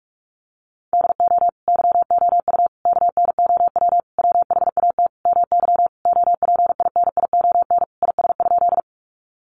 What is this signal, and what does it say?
Morse code, heard as a keyed tone, about 44 wpm, 700 Hz. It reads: DO YOU KNOW WHAT MY OPINION IS?
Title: Sentences from Top 1200 Words 45wpm